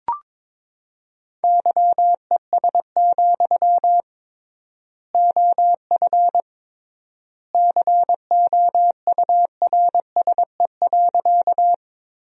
Morse code, heard as a keyed tone, about 22 wpm, 700 Hz. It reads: YES, OF COURSE.
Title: 3x Word-Spacing - Sentences from Top 500 Words - 22wpm